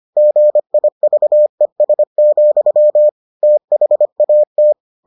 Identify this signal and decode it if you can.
Morse code, heard as a keyed tone, about 25 wpm, 600 Hz. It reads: GIVES, THAT